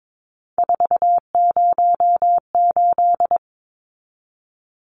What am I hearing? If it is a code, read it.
Morse code, heard as a keyed tone, about 22 wpm, 700 Hz. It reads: 408
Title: Numbers - 3-Digit 22wpm